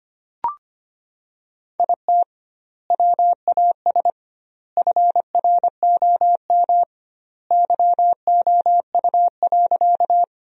Morse code, heard as a keyed tone, about 25 wpm, 700 Hz. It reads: IT WAS FROM YOU.
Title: 2x Word-Spacing - Sentences from Top 100 Words - 25wpm